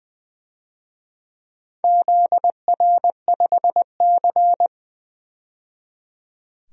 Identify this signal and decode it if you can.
Morse code, heard as a keyed tone, about 20 wpm, 700 Hz. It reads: ZR5C
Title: DXCC-100 Call Signs - 4 Characters - 20wpm